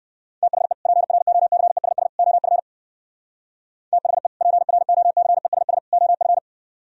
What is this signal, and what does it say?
Morse code, heard as a keyed tone, about 64 wpm, 700 Hz. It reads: THE PROCESS OF THE PROCESS OF